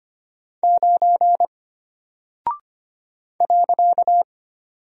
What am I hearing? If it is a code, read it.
Morse code, heard as a keyed tone, about 25 wpm, 700 Hz. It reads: 9 .
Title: Single Character (Letter:Number:Punctuation:Prosign) Alt 25wpm